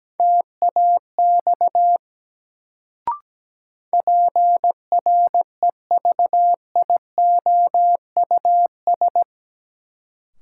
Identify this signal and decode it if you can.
Morse code, heard as a keyed tone, about 17 wpm, 700 Hz. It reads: TAX PREVIOUS
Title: Sentences from Top 1500 Words - New Word Drill Alt - 17wpm